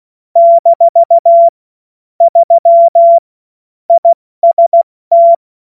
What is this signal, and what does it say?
Morse code, heard as a keyed tone, about 16 wpm, 680 Hz. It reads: - 3 IST
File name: Morsekurs / Witze / Witz 12. April 2022 70 F80